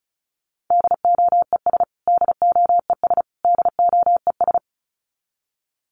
Morse code, heard as a keyed tone, about 34 wpm, 700 Hz. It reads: DOES DOES DOES
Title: Top 500 Words Repeated 35wpm